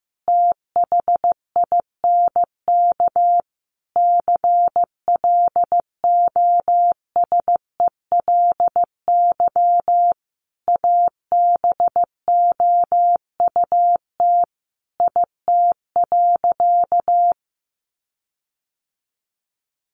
Morse code, heard as a keyed tone, about 14 wpm, 700 Hz. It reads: THINK CLOSELY ABOUT IT.
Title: Sentences from Top 1400 Words 15wpm